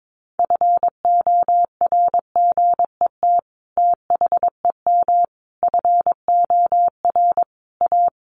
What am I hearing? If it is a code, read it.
Morse code, heard as a keyed tone, about 22 wpm, 700 Hz. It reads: FORGET THEM FOR A
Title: Sentences from Top 1200 Words 22wpm